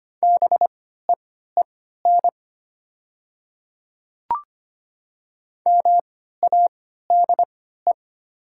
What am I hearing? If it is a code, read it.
Morse code, heard as a keyed tone, about 25 wpm, spaced as for 14 wpm, 700 Hz. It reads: BEEN MADE
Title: 3x Character Spacing - Top 300 Words - 25wpm